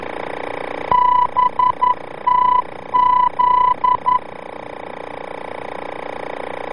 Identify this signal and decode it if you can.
Morse code, heard as a keyed tone, about 11 wpm, 1000 Hz. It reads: BTZ
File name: BTZ-VOR-F